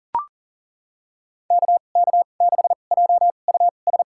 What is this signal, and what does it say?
Morse code, heard as a keyed tone, about 40 wpm, spaced as for 29 wpm, 700 Hz. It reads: KK6JUS
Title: USA Call Signs - 6 Characters - 2x Character Spacing - 40wpm